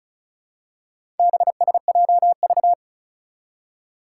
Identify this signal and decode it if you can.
Morse code, heard as a keyed tone, about 36 wpm, 700 Hz. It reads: BSJV
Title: Four Challenging Letter-Number Combinations 35wpm